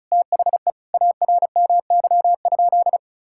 Morse code, heard as a keyed tone, about 36 wpm, 700 Hz. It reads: THE ARMY?
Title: Sentences from Top 1000 Words - Review Part 6 - 35wpm